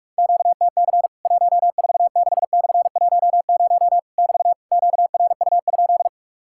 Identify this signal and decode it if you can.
Morse code, heard as a keyed tone, about 46 wpm, 700 Hz. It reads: QTC 146/10 <BT> QRU?
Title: Worked All Europe DX Contest - QTC Exchange - 45wpm